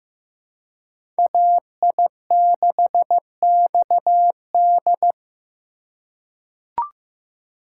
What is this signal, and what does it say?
Morse code, heard as a keyed tone, about 15 wpm, 700 Hz. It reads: AI6XD
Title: USA Call Signs - 5 Characters - 6x Character Spacing - 15wpm